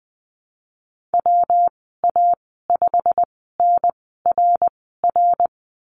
Morse code, heard as a keyed tone, about 20 wpm, spaced as for 14 wpm, 700 Hz. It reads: WA5NRR
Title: 161 Lesson 26g - Letter g - Call Signs 20wpm